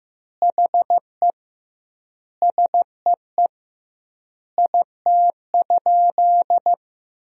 Morse code, heard as a keyed tone, about 15 wpm, 700 Hz. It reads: HE SEE IT?
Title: 2x Word-Spacing - Sentences from Top 100 Words - 15wpm